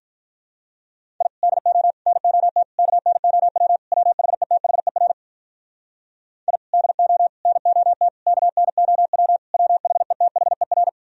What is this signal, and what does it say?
Morse code, heard as a keyed tone, about 46 wpm, 700 Hz. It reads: I DO NOT KNOW WHETHER I DO NOT KNOW WHETHER